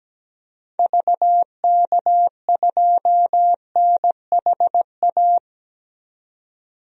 Morse code, heard as a keyed tone, about 17 wpm, 700 Hz. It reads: VK2NHA
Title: DXCC-100 Call Signs - 6 Characters - 17wpm